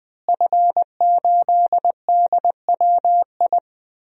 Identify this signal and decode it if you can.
Morse code, heard as a keyed tone, about 20 wpm, 700 Hz. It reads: F8DWI
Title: DXCC-100 Call Signs - 5 Characters - Repeated 3x - 20wpm